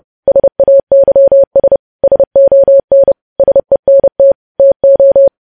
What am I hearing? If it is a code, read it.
Morse code, heard as a keyed tone, about 30 wpm, 550 Hz. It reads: SAYS SON SENT TO
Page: news headlines in morse code at 30 wpm